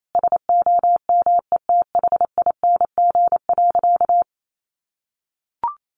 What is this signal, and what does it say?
Morse code, heard as a keyed tone, about 28 wpm, 700 Hz. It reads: SOMETHING.
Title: Sentences from Top 1200 Words 28wpm